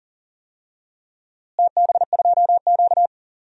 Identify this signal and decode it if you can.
Morse code, heard as a keyed tone, about 37 wpm, 700 Hz. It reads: TB2Q
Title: DXCC-100 Call Signs - 4 Characters - Repeated 3x - 40wpm